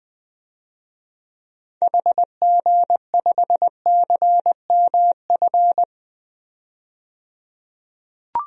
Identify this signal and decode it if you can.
Morse code, heard as a keyed tone, about 20 wpm, 700 Hz. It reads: HG5CMF
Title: DXCC-100 Call Signs - 6 Characters - 20wpm